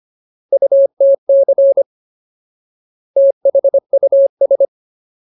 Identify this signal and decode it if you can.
Morse code, heard as a keyed tone, about 25 wpm, 550 Hz. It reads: UTC THUS
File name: Ultimate Word List / 25WPM Ultimate